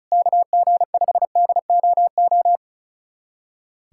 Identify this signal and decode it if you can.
Morse code, heard as a keyed tone, about 34 wpm, 700 Hz. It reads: KG5DOO
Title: USA Callsign DE Callsign - 6 Characters - 35wpm